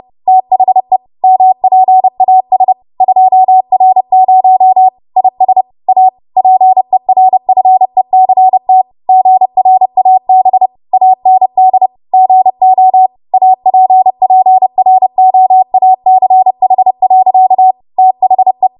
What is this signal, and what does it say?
Morse code, heard as a keyed tone, about 30 wpm, 750 Hz. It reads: THE MPAS 2R0 IS A PERFECT GRAB AND GO APPROACH. THE
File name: Morse / 30WPM